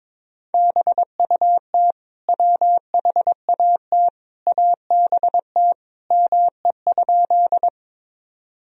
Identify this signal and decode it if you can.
Morse code, heard as a keyed tone, about 22 wpm, 700 Hz. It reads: BUT WHAT ABT ME?
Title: Sentences from Top 200 Words and Abbreviations 22wpm